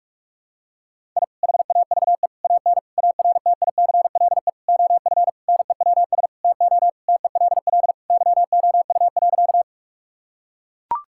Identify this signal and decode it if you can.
Morse code, heard as a keyed tone, about 42 wpm, 700 Hz. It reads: I HAVE AN ARTICLE OF NEWS TO TELL YOU.
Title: Sentences from Top 1000 Words - Review Part 5 - 45wpm